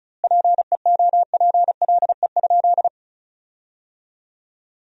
Morse code, heard as a keyed tone, about 36 wpm, 700 Hz. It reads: PEOPLE?
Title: 2x Word-Spacing - Sentences from Top 200 Words - 35wpm